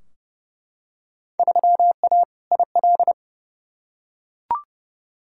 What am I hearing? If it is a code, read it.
Morse code, heard as a keyed tone, about 32 wpm, 700 Hz. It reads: 3A IL
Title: Field Day Exchanges - 30wpm